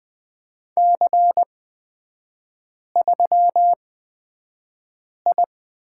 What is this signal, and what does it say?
Morse code, heard as a keyed tone, about 20 wpm, spaced as for 5 wpm, 700 Hz. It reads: C3I
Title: 229 Lesson 37e - Letter x - Three Characters 20wpm